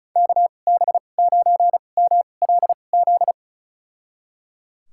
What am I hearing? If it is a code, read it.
Morse code, heard as a keyed tone, about 36 wpm, spaced as for 25 wpm, 700 Hz. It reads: KB9MLZ
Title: USA Call Signs - 6 Characters - 2x Character Spacing - 35wpm